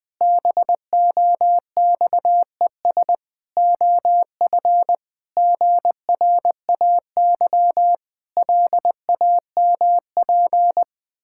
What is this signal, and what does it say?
Morse code, heard as a keyed tone, about 20 wpm, 700 Hz. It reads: BOXES OF GRAY LAMP
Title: Pangram Challenge v2 - 20wpm